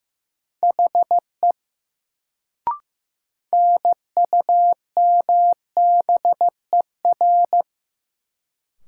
Morse code, heard as a keyed tone, about 15 wpm, 700 Hz. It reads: HE NUMBER
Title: Sentences from Top 100 Words - New Word Drill - 15wpm